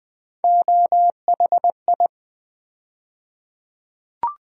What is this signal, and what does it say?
Morse code, heard as a keyed tone, about 20 wpm, 700 Hz. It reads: OHI
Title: USA Call Sign Postfixes - 3 Letters 20wpm